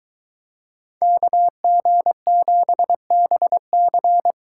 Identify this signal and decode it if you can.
Morse code, heard as a keyed tone, about 23 wpm, 700 Hz. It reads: KG7BC